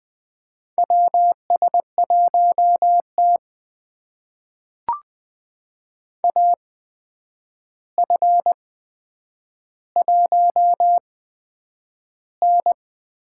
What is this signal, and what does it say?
Morse code, heard as a keyed tone, about 20 wpm, spaced as for 9 wpm, 700 Hz. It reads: WS1T AF1N
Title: USA Call Signs - 4 Characters - 8x Character Spacing - 20wpm